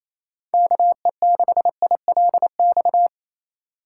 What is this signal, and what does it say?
Morse code, heard as a keyed tone, about 28 wpm, 700 Hz. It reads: KE6ILX
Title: USA Call Signs - 6 Characters - 2x Character Spacing - 28wpm